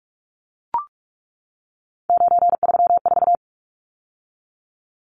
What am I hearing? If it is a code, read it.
Morse code, heard as a keyed tone, about 44 wpm, 700 Hz. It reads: 934
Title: Numbers - 3-Digit 45wpm